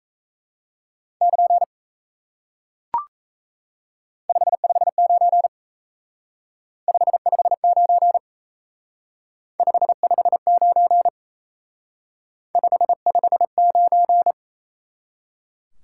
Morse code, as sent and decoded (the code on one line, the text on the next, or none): -.--. / ..... ..... ----. / ..... ..... ----. / ..... ..... ----. / ..... ..... ----.
<KN> 559 559 559 559